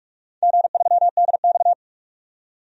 Morse code, heard as a keyed tone, about 46 wpm, 700 Hz. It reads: G3DX
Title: DXCC-100 Call Signs - 4 Characters - 45wpm